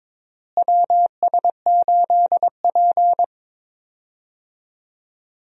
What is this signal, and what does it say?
Morse code, heard as a keyed tone, about 22 wpm, 700 Hz. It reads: WS8P
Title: USA Call Signs - 4 Characters - Repeated 2x - 22wpm